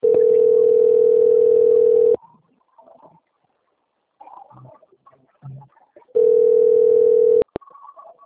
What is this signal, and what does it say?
Ringback tone (North America), 440 + 480 Hz.